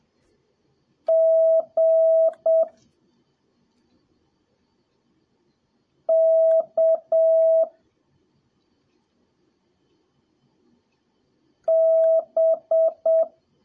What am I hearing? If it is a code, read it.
Morse code, heard as a keyed tone, about 7 wpm, 650 Hz. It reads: G K B
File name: jh8cbh / 2021cw / cw03a